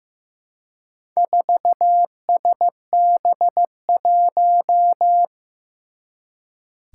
Morse code, heard as a keyed tone, about 15 wpm, 700 Hz. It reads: 4SB1
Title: Four Challenging Letter-Number Combinations 15wpm